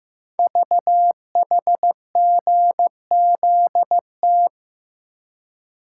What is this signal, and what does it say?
Morse code, heard as a keyed tone, about 15 wpm, 700 Hz. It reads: VHGZT
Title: Code Groups - 5 Letters - 15wpm